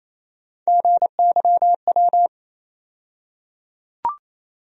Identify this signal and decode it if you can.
Morse code, heard as a keyed tone, about 28 wpm, 700 Hz. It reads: GYW
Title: USA Call Sign Postfixes - 3 Letters 28wpm